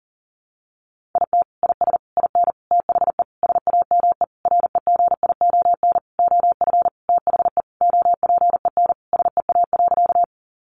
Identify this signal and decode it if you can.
Morse code, heard as a keyed tone, about 40 wpm, 700 Hz. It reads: IT IS IN THE SAME REGION OF THE OPEN SEA.
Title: Sentences from Top 1500 Words 40wpm